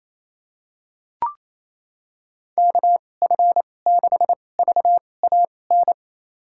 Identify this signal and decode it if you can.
Morse code, heard as a keyed tone, about 28 wpm, spaced as for 20 wpm, 700 Hz. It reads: KF6VAN
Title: USA Call Signs - 2x Character Spacing - 28wpm